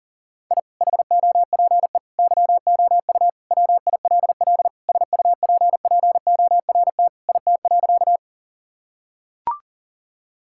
Morse code, heard as a keyed tone, about 40 wpm, 700 Hz. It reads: I HOPE YOU WILL SUPPORT IT.
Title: Sentences from Top 1000 Words - Review Part 5 - 40wpm